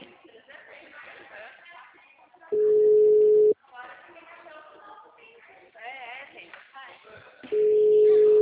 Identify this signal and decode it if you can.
Ringback tone (Europe, 425 Hz), with speech.